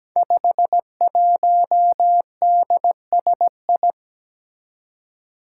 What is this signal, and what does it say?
Morse code, heard as a keyed tone, about 17 wpm, 700 Hz. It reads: 51DSI
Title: Five Challenging Letter-Number Combinations 17wpm